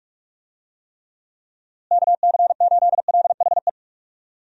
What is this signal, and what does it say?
Morse code, heard as a keyed tone, about 46 wpm, 700 Hz. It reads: KC8LHE
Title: USA Call Signs - 6 Characters - Repeated 2x - 45wpm